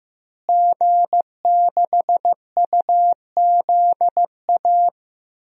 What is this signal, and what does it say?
Morse code, heard as a keyed tone, about 16 wpm, 700 Hz. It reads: G6UZA
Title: DXCC-100 Call Signs - 5 Characters - Repeated 2x - 15wpm